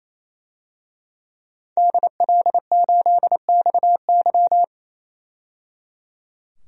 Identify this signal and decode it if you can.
Morse code, heard as a keyed tone, about 28 wpm, 700 Hz. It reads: DL8XY
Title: DXCC-100 Call Signs - 5 Characters - 28wpm